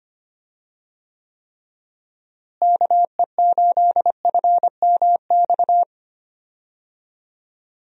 Morse code, heard as a keyed tone, about 25 wpm, 700 Hz. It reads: KE8FMX
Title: USA Call Signs - 6 Characters - Repeated 3x - 25wpm